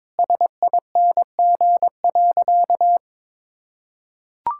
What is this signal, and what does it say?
Morse code, heard as a keyed tone, about 22 wpm, 700 Hz. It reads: SING.
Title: Sentences from Top 1600 Words 22wpm